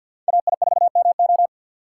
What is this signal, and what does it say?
Morse code, heard as a keyed tone, about 53 wpm, 700 Hz. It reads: AI4MO
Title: USA Call Signs - 5 Characters - 2x Character Spacing - 50wpm